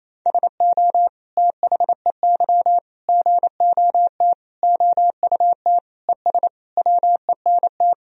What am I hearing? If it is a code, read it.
Morse code, heard as a keyed tone, about 28 wpm, 700 Hz. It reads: SO THEY GOT OUT ES WENT
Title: Sentences from Top 300 Words and Abbreviations 28wpm